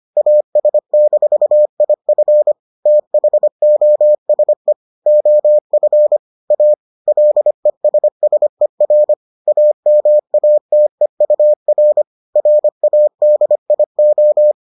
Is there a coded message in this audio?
Yes: Morse code from 0.2 to 14.5 s, heard as a keyed tone, about 25 wpm, 600 Hz. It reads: AS-IF THOSE OF A LESSER AMATEUR RADIO